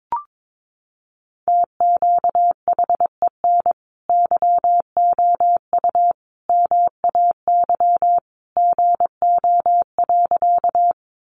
Morse code, heard as a keyed tone, about 22 wpm, 700 Hz. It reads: TQHEN YOU MAY GO.
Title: Sentences from Top 100 Words 22wpm